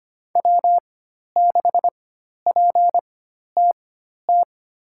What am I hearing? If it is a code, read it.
Morse code, heard as a keyed tone, about 25 wpm, spaced as for 12 wpm, 700 Hz. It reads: W6PTT
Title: USA Call Signs - 5 Characters - 4x Character Spacing - 25wpm